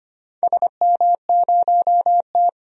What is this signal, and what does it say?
Morse code, heard as a keyed tone, about 25 wpm, 700 Hz. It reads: SM0T
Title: DXCC-100 Call Signs - 4 Characters - 25wpm